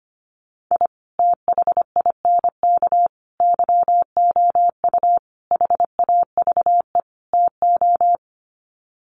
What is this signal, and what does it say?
Morse code, heard as a keyed tone, about 25 wpm, 700 Hz. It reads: I THINK YOU HAVE TO
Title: Sets of 5 Words 25wpm